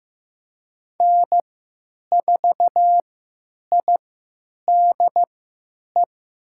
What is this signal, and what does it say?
Morse code, heard as a keyed tone, about 15 wpm, spaced as for 8 wpm, 700 Hz. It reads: N4IDE
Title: USA Call Signs - 3x Character Spacing - 15wpm